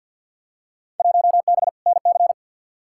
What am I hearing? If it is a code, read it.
Morse code, heard as a keyed tone, about 50 wpm, 700 Hz. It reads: 1B NC
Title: Field Day Exchanges Encore - 50wpm